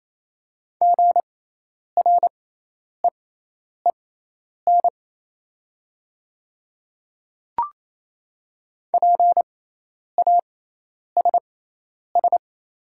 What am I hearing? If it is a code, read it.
Morse code, heard as a keyed tone, about 28 wpm, spaced as for 10 wpm, 700 Hz. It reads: GREEN PASS
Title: 6x Character Spacing - Top 500 Words - 28wpm